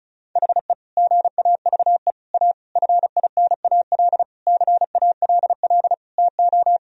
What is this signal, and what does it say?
Morse code, heard as a keyed tone, about 34 wpm, 700 Hz. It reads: HE GAVE A FINAL CALL TO
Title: Sentences from Top 1000 Words - Review Part 2 - 35wpm